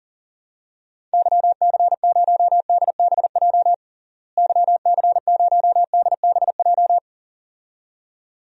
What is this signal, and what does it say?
Morse code, heard as a keyed tone, about 37 wpm, 700 Hz. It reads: YC0DBJ YC0DBJ
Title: DXCC-100 Call Signs - 6 Characters - Repeated 2x - 40wpm